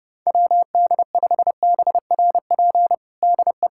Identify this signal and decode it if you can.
Morse code, heard as a keyed tone, about 32 wpm, 700 Hz. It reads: WD5BRP DE
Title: USA Callsign DE Callsign 30wpm